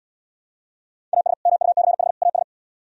Morse code, heard as a keyed tone, about 77 wpm, 700 Hz. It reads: AS MUCH AS